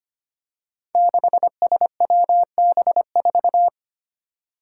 Morse code, heard as a keyed tone, about 25 wpm, 700 Hz. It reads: 6SWB4
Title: Five Challenging Letter-Number Combinations 25wpm